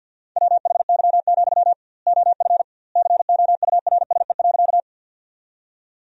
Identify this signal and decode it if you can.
Morse code, heard as a keyed tone, about 51 wpm, 700 Hz. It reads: WHY, OF COURSE.
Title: 2x Word-Spacing - Sentences from Top 500 Words - 50wpm